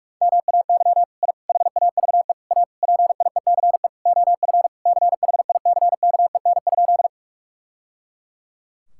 Morse code, heard as a keyed tone, about 46 wpm, 700 Hz. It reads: MAY I HAVE A PIECE OF CHICKEN?
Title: Sentences from Top 1000 Words - Review Part 5 - 45wpm